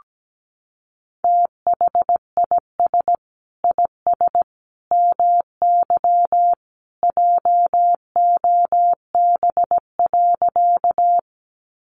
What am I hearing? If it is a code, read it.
Morse code, heard as a keyed tone, about 17 wpm, 700 Hz. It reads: THIS IS MY JOB.
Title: Sentences from Top 600 Words 17wpm